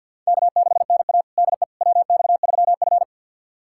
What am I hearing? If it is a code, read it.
Morse code, heard as a keyed tone, about 53 wpm, 700 Hz. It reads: K6NA DE WX3F
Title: USA Callsign DE Callsign - 4 Characters - 50wpm